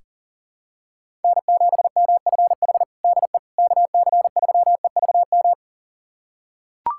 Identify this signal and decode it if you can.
Morse code, heard as a keyed tone, about 37 wpm, 700 Hz. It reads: N7MFH DE KC3EVM
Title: USA Callsign DE Callsign 40wpm